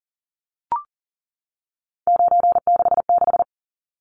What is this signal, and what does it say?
Morse code, heard as a keyed tone, about 40 wpm, 700 Hz. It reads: 966
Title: Numbers - 3-Digit 40wpm